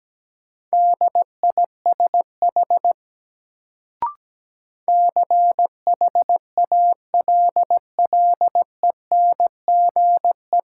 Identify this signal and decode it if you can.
Morse code, heard as a keyed tone, about 17 wpm, 700 Hz. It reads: DISH CHALLENGE
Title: Sentences from Top 1300 Words - New Word Drill Alt - 17wpm